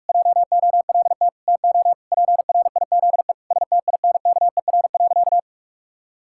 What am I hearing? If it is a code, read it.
Morse code, heard as a keyed tone, about 46 wpm, 700 Hz. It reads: JOLT TO PRIZE STINKER.
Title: Pangram Challenge v2 - 45wpm